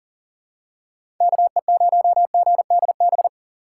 Morse code, heard as a keyed tone, about 40 wpm, 700 Hz. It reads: KE0GDB
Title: USA Call Signs - 6 Characters - 4x Character Spacing - 40wpm